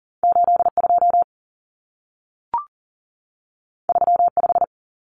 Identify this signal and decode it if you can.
Morse code, heard as a keyed tone, about 40 wpm, 700 Hz. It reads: 82 35
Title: Common Cut Numbers - 2-Digit 40wpm